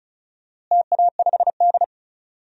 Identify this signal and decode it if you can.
Morse code, heard as a keyed tone, about 36 wpm, 700 Hz. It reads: TA5D
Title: DXCC-100 Call Signs - 4 Characters - Repeated 2x - 35wpm